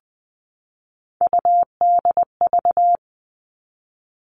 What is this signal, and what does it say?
Morse code, heard as a keyed tone, about 20 wpm, 700 Hz. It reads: UDV
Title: All Cut Numbers - 3-Digit 20wpm